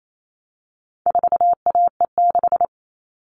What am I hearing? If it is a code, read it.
Morse code, heard as a keyed tone, about 28 wpm, 700 Hz. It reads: 4AE6
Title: All Cut Numbers - 4-Digit 28wpm